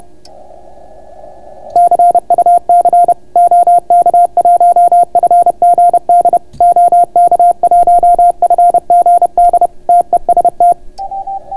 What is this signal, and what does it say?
Morse code, heard as a keyed tone, about 31 wpm, 700 Hz. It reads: CUC OK1FGD OK1FGD TEST